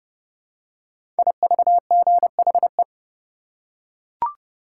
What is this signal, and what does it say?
Morse code, heard as a keyed tone, about 30 wpm, 700 Hz. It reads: IVGHE
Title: Code Groups - 5 Letters - 30wpm